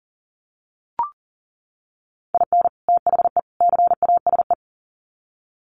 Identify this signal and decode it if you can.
Morse code, heard as a keyed tone, about 40 wpm, 700 Hz. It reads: IN THE CASE
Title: Sets of 3 Words 40wpm